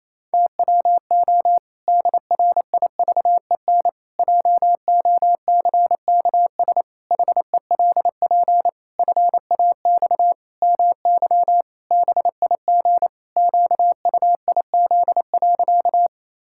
Morse code, heard as a keyed tone, about 28 wpm, 700 Hz. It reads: TWO DRIVEN JOCKS HELP FAX MY BIG QUIZ.
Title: Pangram Challenge v2 - 28wpm